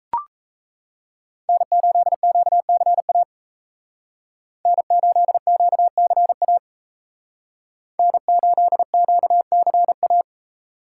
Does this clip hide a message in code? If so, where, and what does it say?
Yes: Morse code from 1.5 to 10.2 s, heard as a keyed tone, about 35 wpm, 700 Hz. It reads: N8QCA N8QCA N8QCA